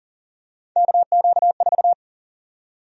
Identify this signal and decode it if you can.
Morse code, heard as a keyed tone, about 37 wpm, 700 Hz. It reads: KQ4
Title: USA Call Sign Prefixes - 40wpm